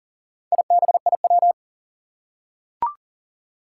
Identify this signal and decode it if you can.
Morse code, heard as a keyed tone, about 41 wpm, 700 Hz. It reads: IBIW
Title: Four Challenging Letter-Number Combinations 40wpm